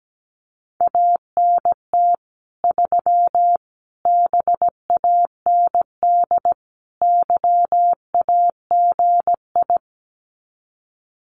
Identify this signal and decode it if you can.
Morse code, heard as a keyed tone, about 17 wpm, 700 Hz. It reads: ANT 3 BAND YAGI
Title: Common QSO Phrases 17wpm